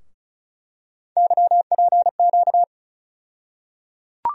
Morse code, heard as a keyed tone, about 36 wpm, 700 Hz. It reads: YPQ
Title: USA Call Sign Postfixes - 3 Letters 35wpm